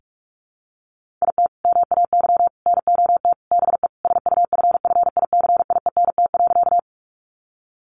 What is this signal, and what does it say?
Morse code, heard as a keyed tone, about 44 wpm, 700 Hz. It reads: IT MAY NOT BE SUFFICIENT.
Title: Sentences from Top 1400 Words 45wpm